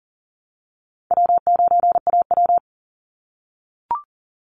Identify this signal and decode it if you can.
Morse code, heard as a keyed tone, about 40 wpm, 700 Hz. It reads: W9AW
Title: Easy Call Signs - 40wpm